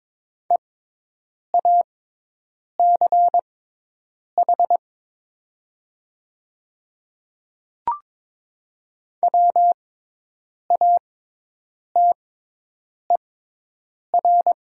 Morse code, heard as a keyed tone, about 22 wpm, spaced as for 8 wpm, 700 Hz. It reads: EACH WATER
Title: 6x Character Spacing - Top 100 Words - 22wpm